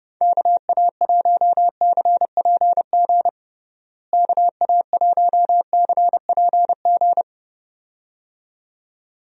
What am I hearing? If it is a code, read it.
Morse code, heard as a keyed tone, about 30 wpm, 700 Hz. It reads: KA1CPG KA1CPG
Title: USA Call Signs - 6 Characters - Repeated 2x - 30wpm